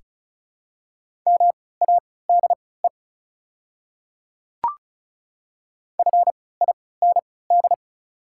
Morse code, heard as a keyed tone, about 36 wpm, spaced as for 20 wpm, 690 Hz. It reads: MADE FIND
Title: 3x Character Spacing - Top 200 Words - 35wpm